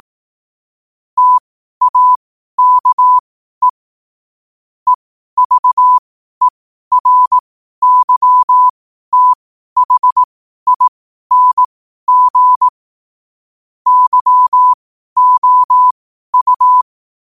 Morse code, heard as a keyed tone, about 18 wpm, spaced as for 12 wpm, 1000 Hz. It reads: TAKE EVERYTHING YOU
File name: qotdfiles / 12wpm